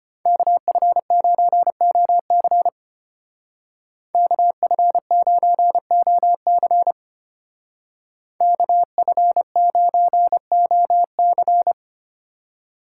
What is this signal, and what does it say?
Morse code, heard as a keyed tone, about 31 wpm, 700 Hz. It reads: KF9OC KF9OC KF9OC
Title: Speed Racing - USA 2x2 Call Signs 25wpm